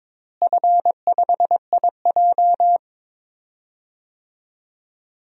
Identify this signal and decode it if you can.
Morse code, heard as a keyed tone, about 22 wpm, 700 Hz. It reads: F5IJ
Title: DXCC-100 Call Signs - 4 Characters - Repeated 2x - 22wpm